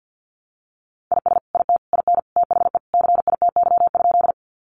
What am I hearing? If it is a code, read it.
Morse code, heard as a keyed tone, about 50 wpm, 700 Hz. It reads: IS IT IN THE CITY?
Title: Sentences from Top 300 Words 50wpm